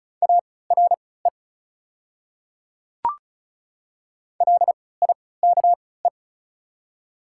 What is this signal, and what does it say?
Morse code, heard as a keyed tone, about 33 wpm, spaced as for 19 wpm, 690 Hz. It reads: ARE LIKE
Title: 3x Character Spacing - Top 100 Words - 35wpm